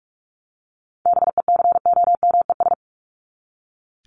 Morse code, heard as a keyed tone, about 44 wpm, 700 Hz. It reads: BECOMES
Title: ES:S Suffix - Top 100 Words with Base Words 45wpm